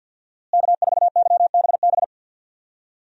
Morse code, heard as a keyed tone, about 52 wpm, 700 Hz. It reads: K4YBB
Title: USA Call Signs - 5 Characters - Repeated 2x - 50wpm